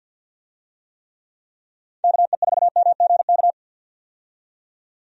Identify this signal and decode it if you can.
Morse code, heard as a keyed tone, about 52 wpm, 700 Hz. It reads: KE4MGK
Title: USA Call Signs - 6 Characters - 50wpm